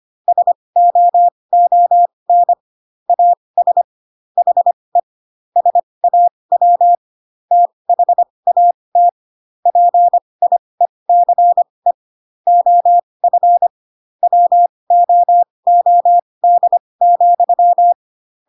Morse code, heard as a keyed tone, about 25 wpm, spaced as for 20 wpm, 700 Hz. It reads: SOON AS HE SAW THAT PIECE OF WOOD,
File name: Pinocchio / Pinocchio ch1 15 25 ik1hge